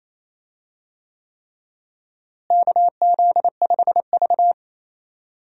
Morse code, heard as a keyed tone, about 28 wpm, 700 Hz. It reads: KZ5V
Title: USA Call Signs - 4 Characters - Repeated 3x - 28wpm